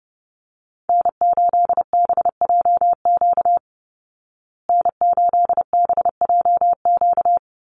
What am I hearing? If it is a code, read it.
Morse code, heard as a keyed tone, about 30 wpm, 700 Hz. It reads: N8BJQ N8BJQ
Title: Call Signs Repeated 3x - 30wpm